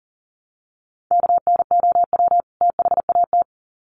Morse code, heard as a keyed tone, about 40 wpm, 700 Hz. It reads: KNOW THAT
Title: Sets of 2 Words 40wpm